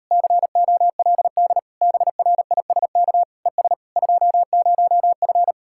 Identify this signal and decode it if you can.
Morse code, heard as a keyed tone, about 38 wpm, 700 Hz. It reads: COLD BRISK ES 20F